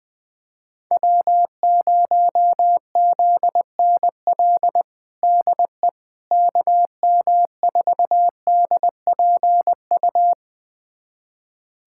Morse code, heard as a keyed tone, about 20 wpm, 700 Hz. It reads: W0ZNL DE KM4DPU
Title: USA Callsign DE Callsign 20wpm